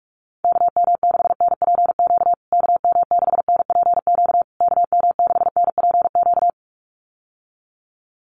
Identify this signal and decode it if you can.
Morse code, heard as a keyed tone, about 44 wpm, 700 Hz. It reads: KM6NPQ KM6NPQ KM6NPQ
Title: USA Call Signs Repeated 3x 45wpm